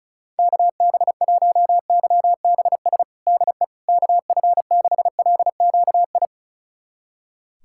Morse code, heard as a keyed tone, about 36 wpm, 700 Hz. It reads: KB1YBS DE KF6LQI
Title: USA Callsign DE Callsign - 6 Characters - 35wpm